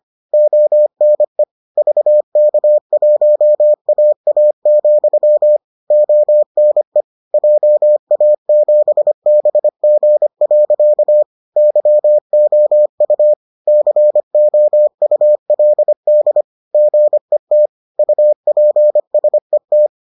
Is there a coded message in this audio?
Yes: Morse code from 0.3 to 19.9 s, heard as a keyed tone, about 25 wpm, 600 Hz. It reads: ONE VK1AA, ONE JA7BG. YOU COULD GET UPSET